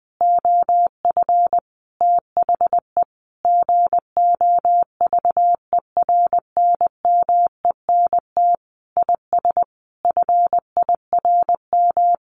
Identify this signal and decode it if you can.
Morse code, heard as a keyed tone, about 20 wpm, 700 Hz. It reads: OF THE GOVERNMENT IS FIRM
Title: Sentences from Top 1500 Words 20wpm